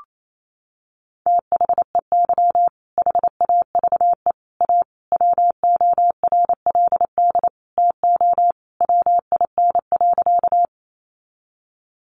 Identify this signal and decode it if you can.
Morse code, heard as a keyed tone, about 28 wpm, 700 Hz. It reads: THEY HAVE A WORLD TO WIN.
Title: Sentences from Top 1500 Words 28wpm